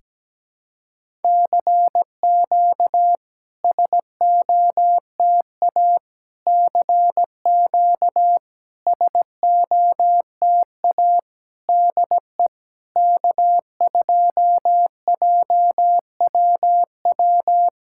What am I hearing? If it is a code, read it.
Morse code, heard as a keyed tone, about 17 wpm, 700 Hz. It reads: CQ SOTA CQ SOTA DE K2JWW
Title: SOTA QSO Practice 17wpm